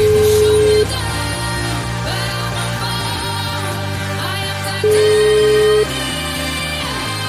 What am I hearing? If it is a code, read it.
Ringback tone (Europe, 425 Hz), with noise.